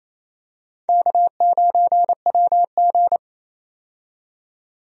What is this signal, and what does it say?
Morse code, heard as a keyed tone, about 27 wpm, 700 Hz. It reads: K9WG
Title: USA Call Signs - 4 Characters - 28wpm